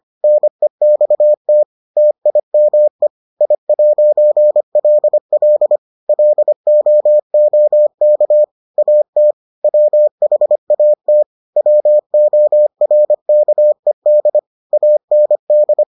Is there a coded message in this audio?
Yes: Morse code from 0.2 to 15.8 s, heard as a keyed tone, about 25 wpm, 600 Hz. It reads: NEXT TIME I'LL LOOK AT WHAT WORKED AND